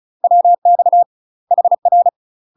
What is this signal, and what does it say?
Morse code, heard as a keyed tone, about 36 wpm, 700 Hz. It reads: WX HR
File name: UK MASTER QSOs TRAINING / 35WPM / UK MASTER QSOs 0036